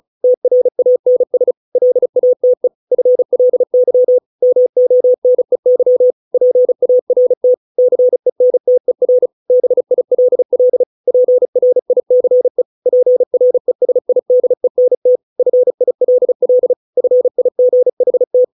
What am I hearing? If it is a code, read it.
Morse code, heard as a keyed tone, about 34 wpm, 500 Hz. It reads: TRANS LATE FLY MONEY PART CENTER BILL PRICE PRESIDENT FILL FIGHT